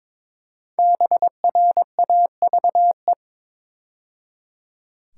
Morse code, heard as a keyed tone, about 22 wpm, 700 Hz. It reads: BRAVE
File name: 5 Letter Words - First 3 Letters Given - 22wpm